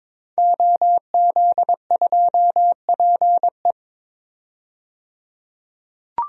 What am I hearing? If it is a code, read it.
Morse code, heard as a keyed tone, about 22 wpm, 700 Hz. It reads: OZ2PE
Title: DXCC-100 Call Signs - 5 Characters - 22wpm